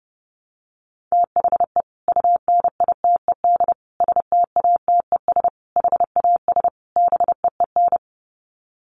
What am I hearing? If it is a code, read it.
Morse code, heard as a keyed tone, about 28 wpm, 700 Hz. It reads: THE UNITED STATES HAS BEEN
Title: Sets of 5 Words Encore2 30wpm